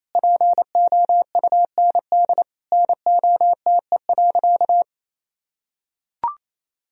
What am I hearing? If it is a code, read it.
Morse code, heard as a keyed tone, about 28 wpm, 700 Hz. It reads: POUND NOTE.
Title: Sentences from Top 1000 Words - Review Part 2 - 28wpm